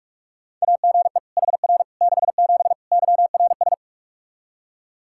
Morse code, heard as a keyed tone, about 41 wpm, 700 Hz. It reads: AGE HR 67 YRS
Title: Practice QSO - Exchange 3 - More Info - 45wpm